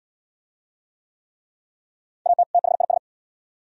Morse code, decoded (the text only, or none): AT THIS